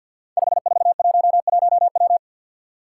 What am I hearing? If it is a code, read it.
Morse code, heard as a keyed tone, about 52 wpm, 700 Hz. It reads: 5411W
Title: Five Challenging Letter-Number Combinations 50wpm